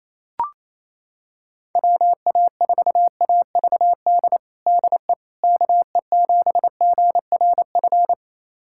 Morse code, heard as a keyed tone, about 28 wpm, 700 Hz. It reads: WA4AVD DE KE7GRF
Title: USA Callsign DE Callsign 28wpm